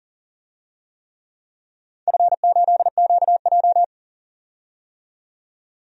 Morse code, heard as a keyed tone, about 40 wpm, 700 Hz. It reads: F8QJ
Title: DXCC-100 Call Signs - 4 Characters - 40wpm